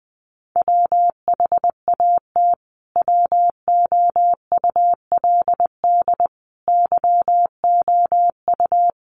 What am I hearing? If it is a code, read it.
Morse code, heard as a keyed tone, about 20 wpm, 700 Hz. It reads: WHAT WOULD YOU
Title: Sentences from Top 1200 Words 20wpm